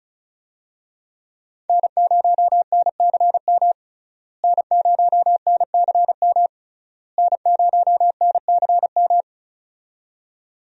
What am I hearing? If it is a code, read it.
Morse code, heard as a keyed tone, about 36 wpm, 700 Hz. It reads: N0NCM N0NCM N0NCM
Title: USA Call Signs - 5 Characters - Repeated 3x - 35wpm